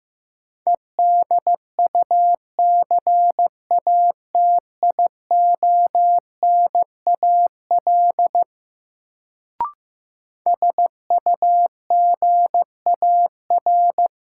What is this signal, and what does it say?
Morse code, heard as a keyed tone, about 15 wpm, 700 Hz. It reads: EDUCATIONAL SUGAR
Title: Sentences from Top 1200 Words - New Word Drill - 15wpm